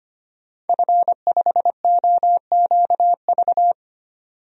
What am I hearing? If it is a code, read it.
Morse code, heard as a keyed tone, about 25 wpm, 700 Hz. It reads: F5OQV
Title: DXCC-100 Call Signs - 5 Characters - 25wpm